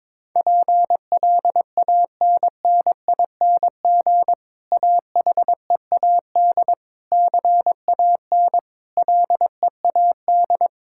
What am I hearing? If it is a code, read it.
Morse code, heard as a keyed tone, about 22 wpm, 700 Hz. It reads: PLANNING AHEAD CAN LEAD
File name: Difficult Sentences with Hints - 22wpm